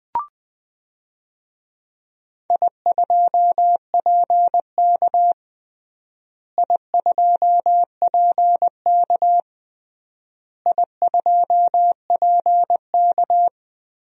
Morse code, heard as a keyed tone, about 20 wpm, 700 Hz. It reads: I2PK I2PK I2PK
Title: DXCC-100 Call Signs - 4 Characters - Repeated 3x - 20wpm